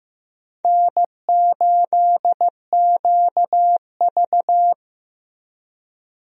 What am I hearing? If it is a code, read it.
Morse code, heard as a keyed tone, about 15 wpm, 700 Hz. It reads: N8QV
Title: USA Call Signs - 4 Characters - Repeated 2x - 15wpm